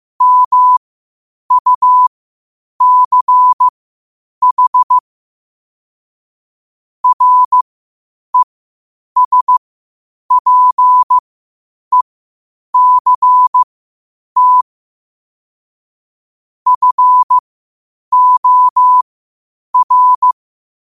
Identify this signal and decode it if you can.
Morse code, heard as a keyed tone, about 15 wpm, spaced as for 8 wpm, 1000 Hz. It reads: MUCH RESPECT FOR
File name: qotdfiles / 8wpm